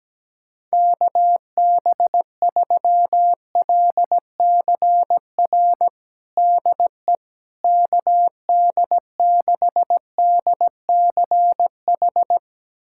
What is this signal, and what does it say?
Morse code, heard as a keyed tone, about 17 wpm, 700 Hz. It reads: KB3LCR DE KD6DCH
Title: USA Callsign DE Callsign - 6 Characters - 17wpm